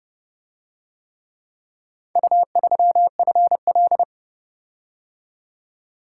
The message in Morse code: U3FL

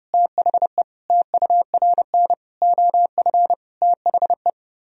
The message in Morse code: THE TURN OF THE

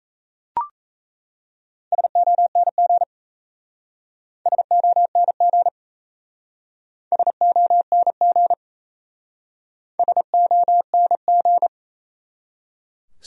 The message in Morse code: SONG SONG SONG SONG

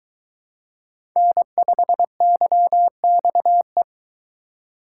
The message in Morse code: N5YXE

